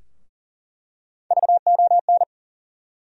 Morse code, heard as a keyed tone, about 40 wpm, 700 Hz. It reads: VON
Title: USA Call Sign Postfixes - 3 Letters 40wpm